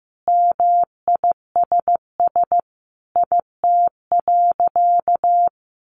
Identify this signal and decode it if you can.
Morse code, heard as a keyed tone, about 15 wpm, 700 Hz. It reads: MISS IT.
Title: A Sound Of Thunder 15wpm